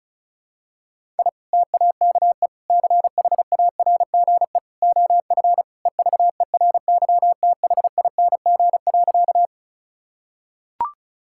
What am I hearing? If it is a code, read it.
Morse code, heard as a keyed tone, about 34 wpm, 700 Hz. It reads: I TAKE CHARGE OF EVERYTHING.
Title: Sentences from Top 1000 Words - Review Part 6 - 35wpm